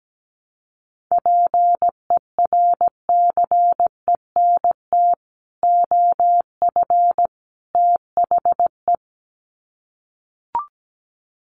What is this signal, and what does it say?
Morse code, heard as a keyed tone, about 17 wpm, 700 Hz. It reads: PERCENT OF THE
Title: Sets of 3 Words 17wpm